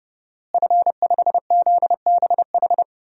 Morse code, heard as a keyed tone, about 30 wpm, 700 Hz. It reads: F5ZBH
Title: DXCC-100 Call Signs - 5 Characters - Repeated 2x - 30wpm